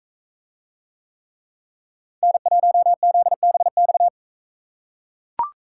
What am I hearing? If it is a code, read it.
Morse code, heard as a keyed tone, about 44 wpm, 700 Hz. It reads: N1ZBX